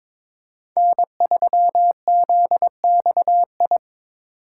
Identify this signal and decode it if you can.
Morse code, heard as a keyed tone, about 22 wpm, 700 Hz. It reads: N3ZXI
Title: USA Call Signs - 5 Characters - 6x Character Spacing - 22wpm